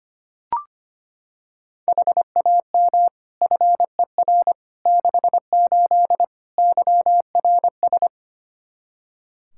Morse code, heard as a keyed tone, about 25 wpm, 700 Hz. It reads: HAM FER 68 YRS
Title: K1USN SST QSO Practice 25wpm